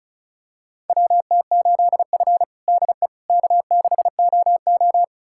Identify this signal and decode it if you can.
Morse code, heard as a keyed tone, about 34 wpm, 700 Hz. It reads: WT8F DE K6OO
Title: USA Callsign DE Callsign - 4 Characters - 35wpm